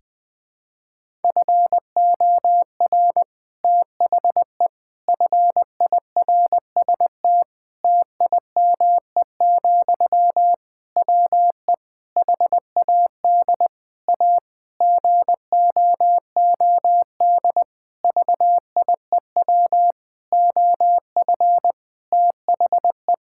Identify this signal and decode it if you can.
Morse code, heard as a keyed tone, about 20 wpm, 700 Hz. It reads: FOR THE FIRST TIME, WE HAD A GOOD VIEW OF THE
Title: Sentences from Top 1000 Words - Review Part 6 - 20wpm